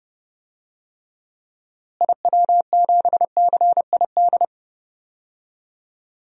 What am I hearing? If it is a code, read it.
Morse code, heard as a keyed tone, about 30 wpm, 700 Hz. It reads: IW7CID
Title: DXCC-100 Call Signs - 6 Characters - Repeated 2x - 30wpm